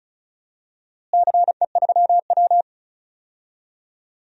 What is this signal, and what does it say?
Morse code, heard as a keyed tone, about 34 wpm, 700 Hz. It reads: CE3W
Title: DXCC-100 Call Signs - 4 Characters - Repeated 3x - 35wpm